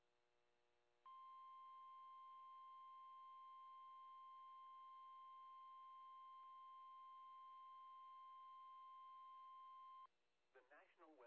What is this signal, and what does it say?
Emergency Alert System (SAME) tones (the 1050 Hz attention tone), with speech. It is heard from 1.0 to 10.1 s.
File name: eas / KIMB / KIMB